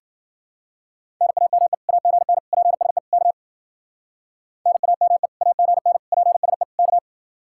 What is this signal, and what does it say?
Morse code, heard as a keyed tone, about 50 wpm, 700 Hz. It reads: NAME AGN PSE K NAME AGN PSE K